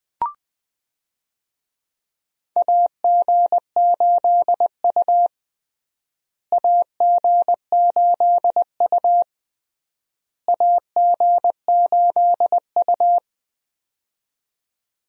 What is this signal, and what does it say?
Morse code, heard as a keyed tone, about 21 wpm, 700 Hz. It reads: AG8U AG8U AG8U
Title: USA Call Signs - 4 Characters - Repeated 3x - 20wpm